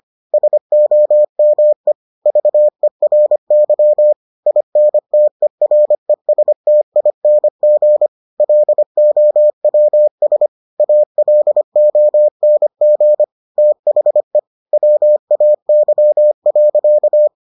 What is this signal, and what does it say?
Morse code, heard as a keyed tone, about 25 wpm, 600 Hz. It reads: SOME VERY INTERESTING LOWS ALONG THE WAY.